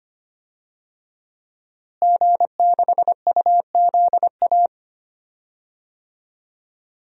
Morse code, heard as a keyed tone, about 25 wpm, 700 Hz. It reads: G6UZA
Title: DXCC-100 Call Signs - 5 Characters - Repeated 2x - 25wpm